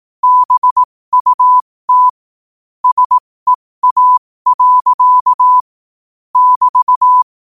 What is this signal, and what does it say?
Morse code, heard as a keyed tone, about 18 wpm, spaced as for 15 wpm, 1000 Hz. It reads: BUT SEA. <BT>
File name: qotdfiles / 15wpm